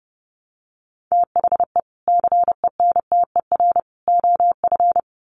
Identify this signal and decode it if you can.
Morse code, heard as a keyed tone, about 30 wpm, 700 Hz. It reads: THE CENTER OF
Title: Sets of 3 Words 30wpm